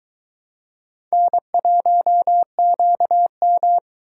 Morse code, heard as a keyed tone, about 21 wpm, 700 Hz. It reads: N1QM